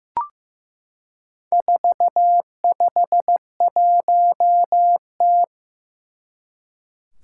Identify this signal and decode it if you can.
Morse code, heard as a keyed tone, about 15 wpm, 700 Hz. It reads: 451T
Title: Four Challenging Letter-Number Combinations 15wpm